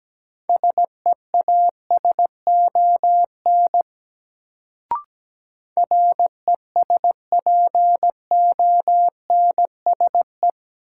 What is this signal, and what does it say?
Morse code, heard as a keyed tone, about 17 wpm, 700 Hz. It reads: SEASON RESPONSE
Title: Sentences from Top 1400 Words - New Word Drill Alt - 17wpm